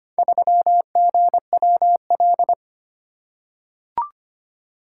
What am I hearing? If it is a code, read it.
Morse code, heard as a keyed tone, about 25 wpm, 700 Hz. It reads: 3GWL
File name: practice / 25wpm / Four Letter-Number Combinations - 25wpm